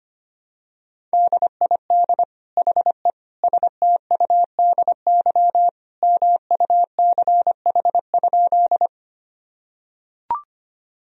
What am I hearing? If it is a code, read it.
Morse code, heard as a keyed tone, about 25 wpm, 700 Hz. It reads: DID HE STUDY MUCH?
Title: Sentences from Top 1000 Words - Review Part 1 - 25wpm